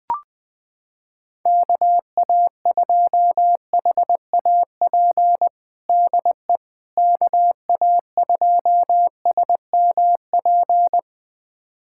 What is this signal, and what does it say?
Morse code, heard as a keyed tone, about 21 wpm, 700 Hz. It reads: KA2HAP DE KA2SMP
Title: USA Callsign DE Callsign - 6 Characters - 20wpm